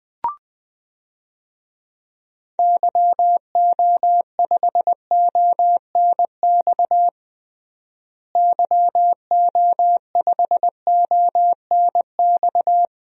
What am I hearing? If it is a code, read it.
Morse code, heard as a keyed tone, about 21 wpm, 700 Hz. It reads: YO5ONX YO5ONX
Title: DXCC-100 Call Signs - 6 Characters - Repeated 2x - 20wpm